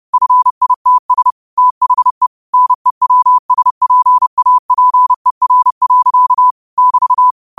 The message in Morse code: FITS THE NEWSPAPER. <BT>

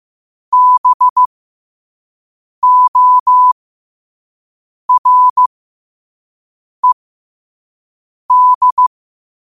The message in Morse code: BORED

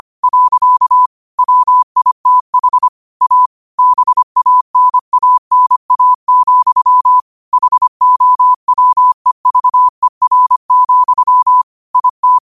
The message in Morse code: . WITH A BANANA, HOWEVER, IT

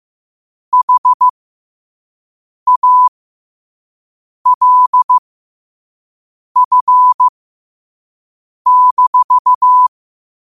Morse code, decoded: HALF-

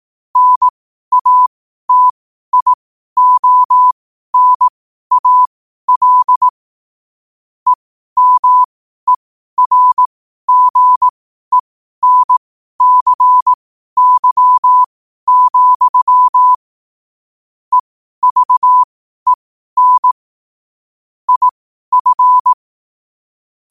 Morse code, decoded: NATIONAL EMERGENCY, EVEN IF

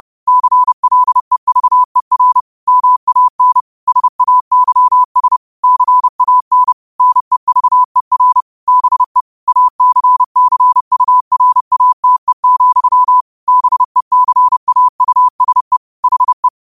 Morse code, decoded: CLEVER MAN SAYS CAN NEVER BE ACCURATE, BECAUSE HE

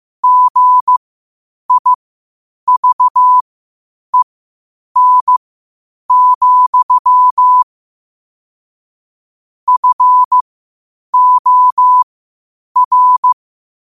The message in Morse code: GIVEN, FOR

